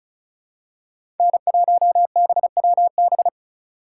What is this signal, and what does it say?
Morse code, heard as a keyed tone, about 36 wpm, 700 Hz. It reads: N1BWB
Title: USA Call Signs - 5 Characters - 2x Character Spacing - 35wpm